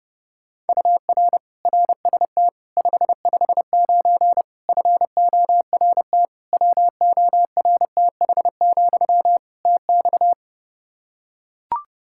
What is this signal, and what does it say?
Morse code, heard as a keyed tone, about 30 wpm, 700 Hz. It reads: UR RST 559 FORT WORTH, TX
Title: Practice QSO - Exchange 1 - Beginning - 30wpm